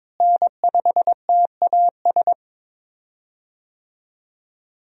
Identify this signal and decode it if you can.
Morse code, heard as a keyed tone, about 22 wpm, 700 Hz. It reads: N5TAS
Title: USA Call Signs - 5 Characters - Repeated 3x - 22wpm